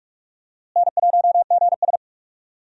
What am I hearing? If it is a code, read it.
Morse code, heard as a keyed tone, about 42 wpm, 700 Hz. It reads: N1GS
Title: USA Call Signs - 4 Characters - Repeated 2x - 45wpm